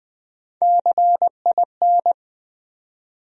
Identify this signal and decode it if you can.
Morse code, heard as a keyed tone, about 21 wpm, 700 Hz. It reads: CIN
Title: USA Call Sign Postfixes - 3 Letters 20wpm